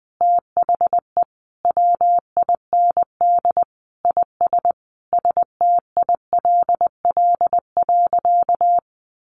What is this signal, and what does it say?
Morse code, heard as a keyed tone, about 20 wpm, 700 Hz. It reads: THE WIND IS STILL.
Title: Sentences from Top 1200 Words 20wpm